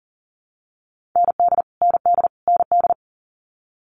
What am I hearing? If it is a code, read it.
Morse code, heard as a keyed tone, about 39 wpm, 700 Hz. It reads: ND ND ND
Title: Multi-letter Phonemes Repeated 3x 40wpm